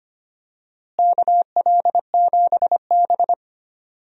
Morse code, heard as a keyed tone, about 25 wpm, 700 Hz. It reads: KL7B
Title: DXCC-100 Call Signs - 4 Characters - Repeated 2x - 25wpm